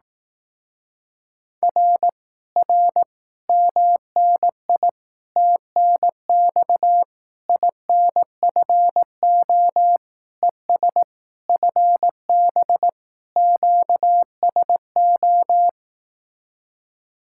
Morse code, heard as a keyed tone, about 18 wpm, 700 Hz. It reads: R R MNI TNX INFO ES FB QSO
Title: Speed-Racing Common QSO Phrases Encore - 15wpm